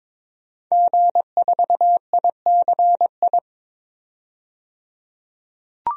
Morse code, heard as a keyed tone, about 22 wpm, 700 Hz. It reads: G4ICI
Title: DXCC-100 Call Signs - 5 Characters - 22wpm